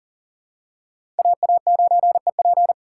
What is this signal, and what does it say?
Morse code, heard as a keyed tone, about 40 wpm, 700 Hz. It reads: AA9EP
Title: USA Call Signs - 5 Characters - 6x Character Spacing - 40wpm